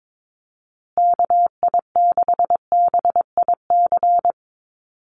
Morse code, heard as a keyed tone, about 22 wpm, 700 Hz. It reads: KI6BIC
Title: USA Call Signs Repeated 3x 22wpm